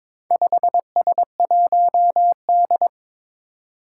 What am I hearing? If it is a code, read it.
Morse code, heard as a keyed tone, about 23 wpm, 700 Hz. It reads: 5S1D
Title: Four Challenging Letter-Number Combinations 22wpm